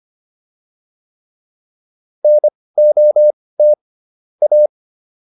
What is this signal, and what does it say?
Morse code, heard as a keyed tone, about 25 wpm, spaced as for 18 wpm, 600 Hz. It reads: NOT A